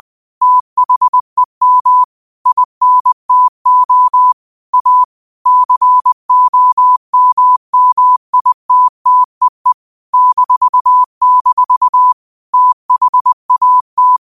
Morse code, heard as a keyed tone, about 20 wpm, 1000 Hz. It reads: THEM INTO A COMMITTEE -- THAT
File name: qotdfiles / 20wpm